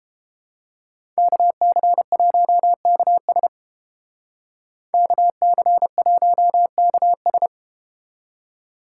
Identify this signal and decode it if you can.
Morse code, heard as a keyed tone, about 30 wpm, 700 Hz. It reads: KC1KS KC1KS